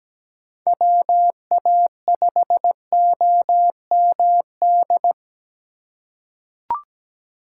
Morse code, heard as a keyed tone, about 17 wpm, 700 Hz. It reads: WA5OMD
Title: USA Call Signs - 3x Character Spacing - 17wpm